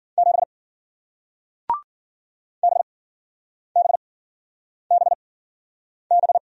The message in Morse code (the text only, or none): 7 BBBB